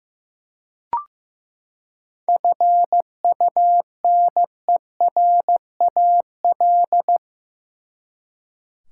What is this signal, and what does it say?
Morse code, heard as a keyed tone, about 15 wpm, 700 Hz. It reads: FUNERAL
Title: Seven Letter Words - 15wpm